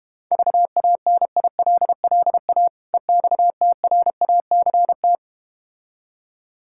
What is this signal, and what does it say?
Morse code, heard as a keyed tone, about 32 wpm, 700 Hz. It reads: VANILLA EXTRACT